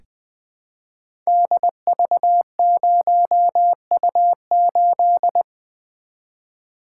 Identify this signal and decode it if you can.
Morse code, heard as a keyed tone, about 20 wpm, 700 Hz. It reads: DV0U8
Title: Code Groups - 5 Letter:Numbers - 20wpm